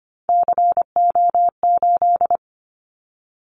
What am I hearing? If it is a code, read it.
Morse code, heard as a keyed tone, about 25 wpm, 700 Hz. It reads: CO8
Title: Callsign Prefixes for Top 100 DXCC - 25wpm